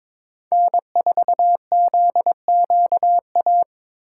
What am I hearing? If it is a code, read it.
Morse code, heard as a keyed tone, about 26 wpm, 700 Hz. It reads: N4ZQA
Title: Speed Racing - USA 1x3 Call Signs 22wpm